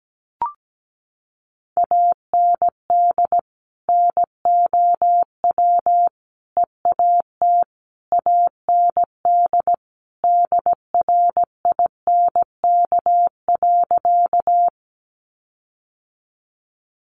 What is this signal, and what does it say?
Morse code, heard as a keyed tone, about 17 wpm, 700 Hz. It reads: AND NOW EAT AND DRINK.
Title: Sentences from Top 1500 Words 17wpm